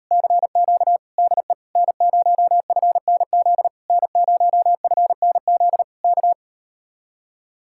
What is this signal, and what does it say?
Morse code, heard as a keyed tone, about 39 wpm, 700 Hz. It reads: CQ DE N0FNZ N0FNZ K